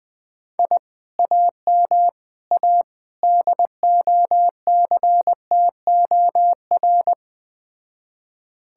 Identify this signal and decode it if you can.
Morse code, heard as a keyed tone, about 20 wpm, 700 Hz. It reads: I AM A DOCTOR
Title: Speed-Racing Common QSO Phrases - 17wpm